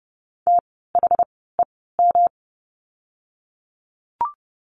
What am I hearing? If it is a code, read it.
Morse code, heard as a keyed tone, about 30 wpm, spaced as for 17 wpm, 700 Hz. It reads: THEM
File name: course / 30:10wpm / 230 Lesson 37f - Letter x - Words 30wpm